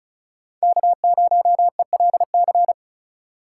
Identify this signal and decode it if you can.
Morse code, heard as a keyed tone, about 43 wpm, 700 Hz. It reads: K0ELC
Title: Speed Racing - USA Call Signs 35wpm